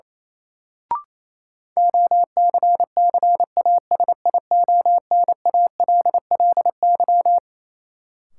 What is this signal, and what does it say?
Morse code, heard as a keyed tone, about 28 wpm, 700 Hz. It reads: OCCASIONALLY
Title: Sentences from Top 1300 Words - New Word Drill - 28wpm